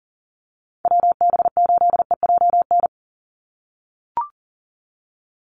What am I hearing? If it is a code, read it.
Morse code, heard as a keyed tone, about 40 wpm, 700 Hz. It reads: WB8EJN
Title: Call Signs - 40wpm